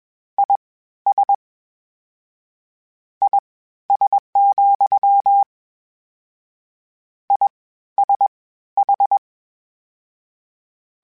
Morse code, heard as a keyed tone, about 21 wpm, spaced as for 13 wpm, 800 Hz. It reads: IS IS, ISH
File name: General / General-mp3s / Suffixes sp @20